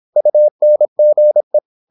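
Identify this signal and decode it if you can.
Morse code, heard as a keyed tone, about 26 wpm, 600 Hz. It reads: UNGE
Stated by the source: High-speed (QRQ) Morse at 26 wpm.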